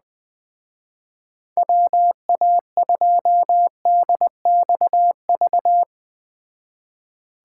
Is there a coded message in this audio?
Yes: Morse code from 1.6 to 5.8 s, heard as a keyed tone, about 20 wpm, 700 Hz. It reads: WA2DXV